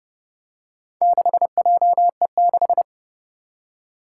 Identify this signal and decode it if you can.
Morse code, heard as a keyed tone, about 30 wpm, 700 Hz. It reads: 6JE6
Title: Four Challenging Letter-Number Combinations 30wpm